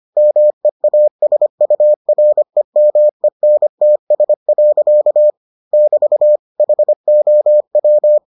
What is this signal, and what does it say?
Morse code, heard as a keyed tone, about 25 wpm, 600 Hz. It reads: MEASUREMENTS. <BT> HOW